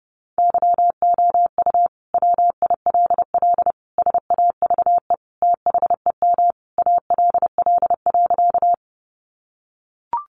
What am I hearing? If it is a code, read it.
Morse code, heard as a keyed tone, about 30 wpm, 700 Hz. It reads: YOU WILL SAVE THEM ALL.
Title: Sentences from Top 1100 Words 30wpm